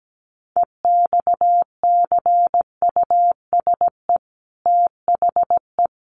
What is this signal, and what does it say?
Morse code, heard as a keyed tone, about 17 wpm, 700 Hz. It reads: EXCUSE THE
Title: Sentences from Top 800 Words 17wpm